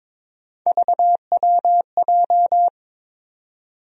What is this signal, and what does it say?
Morse code, heard as a keyed tone, about 22 wpm, 700 Hz. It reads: VWJ
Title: USA Call Sign Postfixes - 3 Letters 22wpm